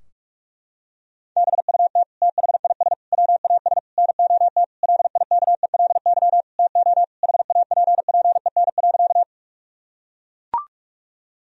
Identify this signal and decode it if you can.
Morse code, heard as a keyed tone, about 41 wpm, 700 Hz. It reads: BUT THIS WAS NOT LIKELY TO HAPPEN.
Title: Sentences from Top 1000 Words - Review Part 4 - 45wpm